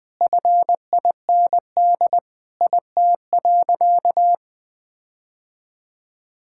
Morse code, heard as a keyed tone, about 20 wpm, 700 Hz. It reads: FIND IT.
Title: Sentences with 4-Word Phrases - Spoken Beginning - 20wpm